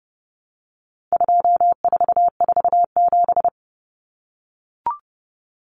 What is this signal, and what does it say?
Morse code, heard as a keyed tone, about 30 wpm, 700 Hz. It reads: 2447
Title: Numbers - 4-Digit 30wpm